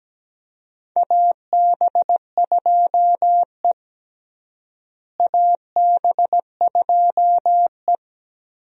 Morse code, heard as a keyed tone, about 17 wpm, 700 Hz. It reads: AB2E AB2E
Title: USA Call Signs - 4 Characters - Repeated 2x - 17wpm